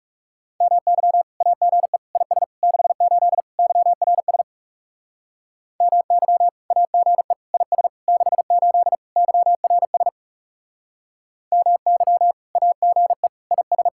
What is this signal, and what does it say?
Morse code, heard as a keyed tone, about 38 wpm, 700 Hz. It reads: MY AGE IS 68 YRS MY AGE IS 68 YRS MY AGE IS